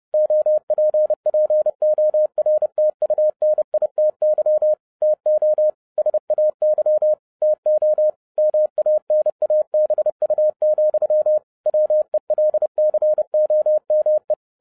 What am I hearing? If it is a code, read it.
Morse code, heard as a keyed tone, about 30 wpm, 610 Hz. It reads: OPPORTUNITY TO SAY TO MANABU, WELCOME
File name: 30wpm